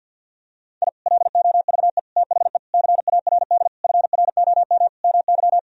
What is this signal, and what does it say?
Morse code, heard as a keyed tone, about 48 wpm, 700 Hz. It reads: I LOVE THE CARD FROM MY
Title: Sentences from Top 1000 Words - Review Part 4 - 50wpm